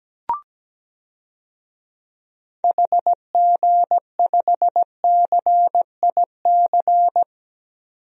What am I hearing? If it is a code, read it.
Morse code, heard as a keyed tone, about 17 wpm, 700 Hz. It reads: HG5CIC
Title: DXCC-100 Call Signs - 6 Characters - 17wpm